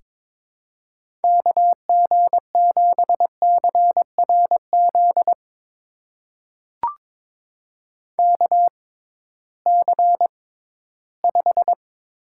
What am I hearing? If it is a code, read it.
Morse code, heard as a keyed tone, about 22 wpm, spaced as for 14 wpm, 700 Hz. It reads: KG7CRZ KC5
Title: USA Call Signs - 6 Characters - 6x Character Spacing - 22wpm